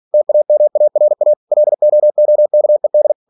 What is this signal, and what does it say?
Morse code, heard as a keyed tone, about 46 wpm, 600 Hz. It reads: TAMARA LOOKED
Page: Never 47wpm